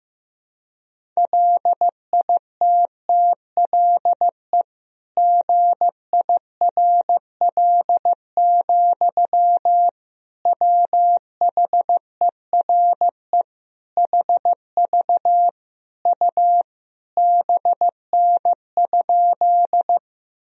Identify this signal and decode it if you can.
Morse code, heard as a keyed tone, about 15 wpm, 700 Hz. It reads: LITTLE GIRL, WHERE HV U BN?
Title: Sentences from Top 400 Words and Abbreviations 15wpm